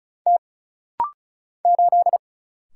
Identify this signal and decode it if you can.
Morse code, heard as a keyed tone, about 36 wpm, 700 Hz. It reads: T 8
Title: Single Letter-Number Alt - Rapid-Fire 35wpm